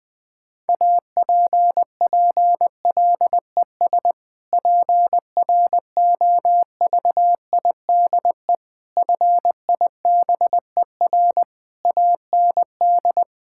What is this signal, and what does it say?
Morse code, heard as a keyed tone, about 20 wpm, 700 Hz. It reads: APPLES PROVIDE FIBER AND
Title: Difficult Sentences with Hints - 20wpm